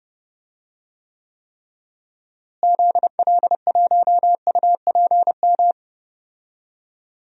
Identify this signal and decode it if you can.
Morse code, heard as a keyed tone, about 30 wpm, 700 Hz. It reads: ZL1UPM
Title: DXCC-100 Call Signs - 6 Characters - Repeated 3x - 30wpm